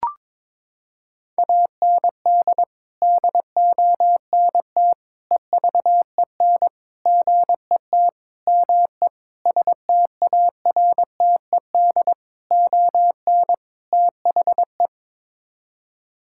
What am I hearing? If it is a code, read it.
Morse code, heard as a keyed tone, about 21 wpm, 700 Hz. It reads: AND DONT EVEN GET ME STARTED ON THE
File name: practice / 22wpm / Sets of 8 Words - 22wpm